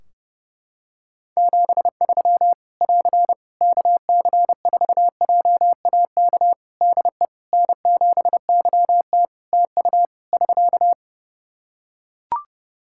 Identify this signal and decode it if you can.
Morse code, heard as a keyed tone, about 32 wpm, 700 Hz. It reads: 73 <AR> KC4JAK DE N7YT TU <SK>
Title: Practice QSO - Exchange 4 - Ending - 30wpm